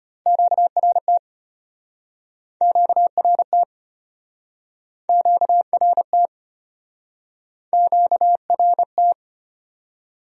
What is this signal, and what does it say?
Morse code, heard as a keyed tone, about 33 wpm, 700 Hz. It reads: QRT QRT QRT QRT